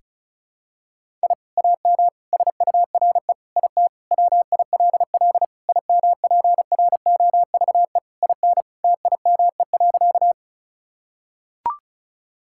Morse code, heard as a keyed tone, about 36 wpm, 700 Hz. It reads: I AM SURE IT WILL IMPROVE IN TIME.
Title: Sentences from Top 1000 Words - Review Part 5 - 35wpm